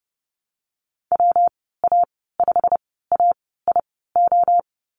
Morse code, heard as a keyed tone, about 30 wpm, spaced as for 17 wpm, 700 Hz. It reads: WA5AIO
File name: Lesson 19g - Letter w - Call Signs 30wpm